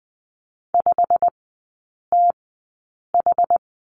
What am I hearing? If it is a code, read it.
Morse code, heard as a keyed tone, about 20 wpm, spaced as for 8 wpm, 700 Hz. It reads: 5TH
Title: 107 Lesson 18e - Letter m - Three Characters 20wpm